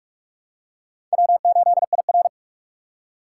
Morse code, heard as a keyed tone, about 42 wpm, 700 Hz. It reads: W8IR